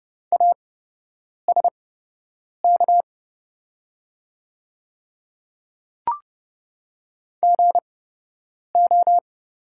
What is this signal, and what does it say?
Morse code, heard as a keyed tone, about 29 wpm, spaced as for 8 wpm, 700 Hz. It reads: ASK GO
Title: 8x Character Spacing - Top 200 Words - 30wpm